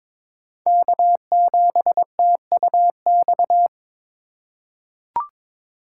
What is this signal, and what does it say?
Morse code, heard as a keyed tone, about 22 wpm, 700 Hz. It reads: K7TUX
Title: USA Call Signs - 5 Characters - 2x Character Spacing - 22wpm